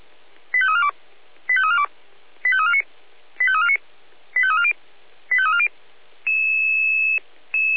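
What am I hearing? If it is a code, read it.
Five-tone selective call (ZVEI), heard as quick runs of five tones, tones of about 70 ms. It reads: 85342 85342 85349 85349 85340 85340
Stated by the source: ZVEI five-tone call received by SDR.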